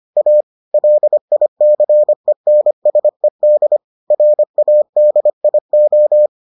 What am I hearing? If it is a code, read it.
Morse code, heard as a keyed tone, about 25 wpm, 600 Hz. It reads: A LICENSED RADIO